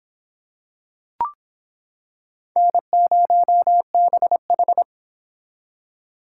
Morse code, heard as a keyed tone, about 25 wpm, 700 Hz. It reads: N0BH